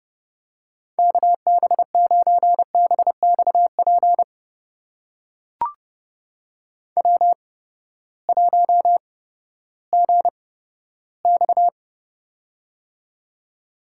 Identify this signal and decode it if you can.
Morse code, heard as a keyed tone, about 30 wpm, spaced as for 15 wpm, 700 Hz. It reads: KB9BXP W1GX
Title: USA Call Signs - 8x Character Spacing - 30wpm